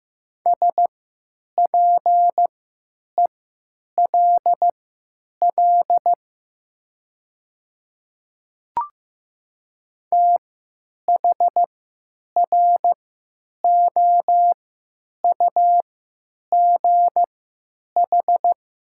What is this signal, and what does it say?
Morse code, heard as a keyed tone, about 15 wpm, spaced as for 9 wpm, 700 Hz. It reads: SPELL THROUGH
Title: 3x Character Spacing - Top 300 Words - 15wpm